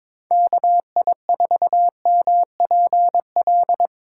Morse code, heard as a keyed tone, about 22 wpm, 700 Hz. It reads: KI4MPL
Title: USA Call Signs - 6 Characters - 2x Character Spacing - 22wpm